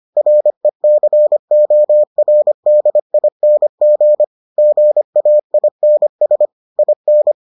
Morse code, heard as a keyed tone, about 25 wpm, 600 Hz. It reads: RECORDING GAINS IN